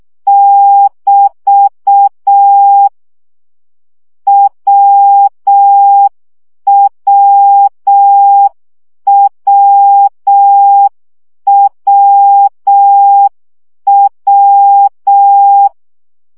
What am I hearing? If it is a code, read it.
Morse code, heard as a keyed tone, about 6 wpm, 800 Hz. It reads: <BT> WWWWW
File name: CW gyakorlas w 5 baud